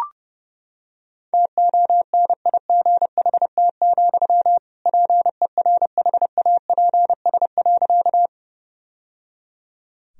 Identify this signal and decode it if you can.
Morse code, heard as a keyed tone, about 30 wpm, 700 Hz. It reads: TONIGHT, PERHAPS.
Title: Sentences from Top 1000 Words - Review Part 3 - 30wpm